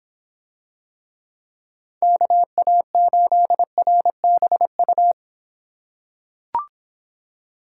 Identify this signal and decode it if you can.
Morse code, heard as a keyed tone, about 24 wpm, 700 Hz. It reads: KA8RBU